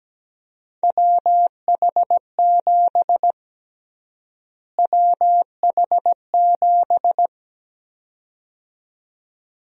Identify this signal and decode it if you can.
Morse code, heard as a keyed tone, about 17 wpm, 700 Hz. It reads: WH7 WH7
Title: USA Call Sign Prefixes - Repeated 2x - 17wpm